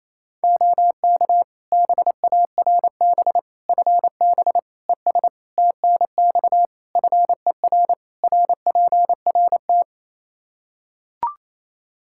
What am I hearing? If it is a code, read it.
Morse code, heard as a keyed tone, about 28 wpm, 700 Hz. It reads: OK BARB FB ES TNX FER RPRT
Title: Practice QSO - Exchange 2 - Info - 28wpm